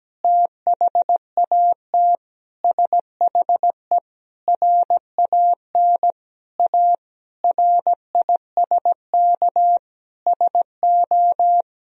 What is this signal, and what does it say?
Morse code, heard as a keyed tone, about 17 wpm, 700 Hz. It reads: THAT SHE RAN A RISK SO
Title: Sentences from Top 1000 Words - Review Part 4 - 17wpm